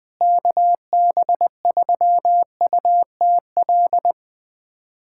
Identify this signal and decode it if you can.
Morse code, heard as a keyed tone, about 19 wpm, 700 Hz. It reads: KB3UTL
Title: USA Callsign DE Callsign 20wpm